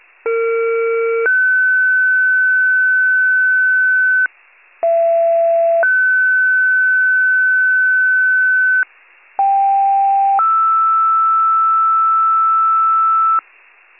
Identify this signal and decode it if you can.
Two-tone page, heard as a one-second tone, then a three-second tone, about 460 then 1500 Hz, about 670 then 1500 Hz, about 790 then 1320 Hz. It is heard from 0.2 to 13.4 s.
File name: pager / 39 FIRE / wcalert